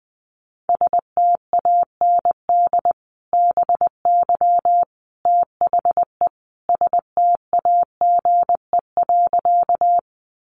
Morse code, heard as a keyed tone, about 20 wpm, 700 Hz. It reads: STAND BY THE STAGE.
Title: Sentences from Top 1200 Words 20wpm